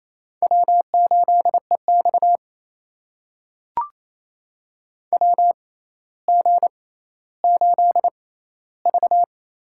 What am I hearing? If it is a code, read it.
Morse code, heard as a keyed tone, about 27 wpm, spaced as for 14 wpm, 700 Hz. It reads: W8EX WG8V
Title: USA Call Signs - 4 Characters - 6x Character Spacing - 28wpm